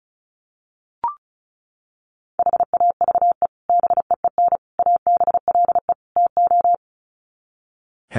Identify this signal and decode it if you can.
Morse code, heard as a keyed tone, about 34 wpm, 700 Hz. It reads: HAVE BEEN ABLE TO
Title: Sets of 4 Words 35wpm